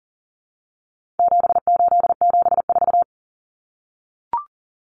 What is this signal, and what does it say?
Morse code, heard as a keyed tone, about 40 wpm, 700 Hz. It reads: 7874
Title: Common Cut Numbers - 4-Digit 40wpm